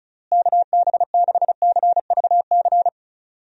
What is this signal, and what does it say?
Morse code, heard as a keyed tone, about 36 wpm, 700 Hz. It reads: KB6CVC
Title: USA Call Signs - 6 Characters - Repeated 2x - 35wpm